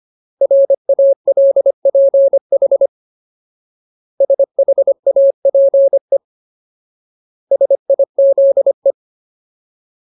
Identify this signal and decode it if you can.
Morse code, heard as a keyed tone, about 25 wpm, 550 Hz. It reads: RALPH SHAPE SIZE
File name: Ultimate Word List / 25WPM Ultimate